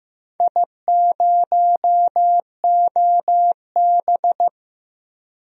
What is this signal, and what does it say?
Morse code, heard as a keyed tone, about 15 wpm, 700 Hz. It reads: I0OB
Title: DXCC-100 Call Signs - 4 Characters - Repeated 3x - 15wpm